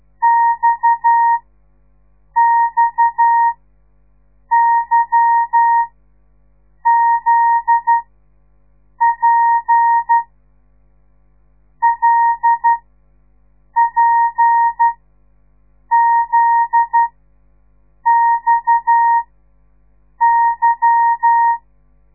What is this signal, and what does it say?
Morse code, heard as a keyed tone, about 12 wpm, spaced as for 6 wpm, 930 Hz. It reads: XXYZP LPZXY
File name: radiomorningstar / morse / les 8